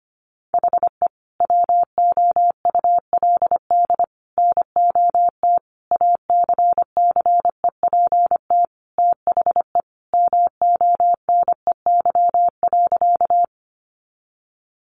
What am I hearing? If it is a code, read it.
Morse code, heard as a keyed tone, about 25 wpm, 700 Hz. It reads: HE WOULD NOT ACCEPT THE MONEY.
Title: Sentences from Top 1300 Words 25wpm